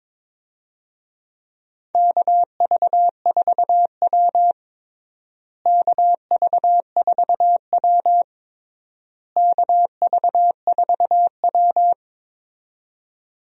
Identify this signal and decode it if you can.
Morse code, heard as a keyed tone, about 21 wpm, 700 Hz. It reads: KV4W KV4W KV4W
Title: USA Call Signs - 4 Characters - Repeated 3x - 22wpm